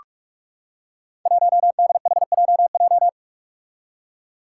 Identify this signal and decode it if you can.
Morse code, heard as a keyed tone, about 44 wpm, 700 Hz. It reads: 1DHJJ
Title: Five Challenging Letter-Number Combinations 45wpm